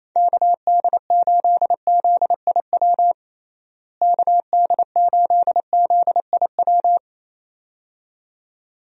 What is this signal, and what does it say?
Morse code, heard as a keyed tone, about 28 wpm, 700 Hz. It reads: KD8ZIW KD8ZIW
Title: USA Call Signs - 6 Characters - Repeated 2x - 28wpm